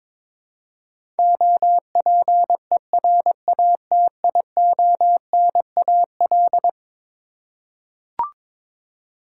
Morse code, heard as a keyed tone, about 22 wpm, 700 Hz. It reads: OPERATIONAL
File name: practice / 22wpm / Eleven Letter Words - 22wpm